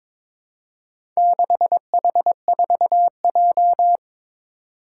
Morse code, heard as a keyed tone, about 23 wpm, 700 Hz. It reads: 6H4J
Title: Four Challenging Letter-Number Combinations 22wpm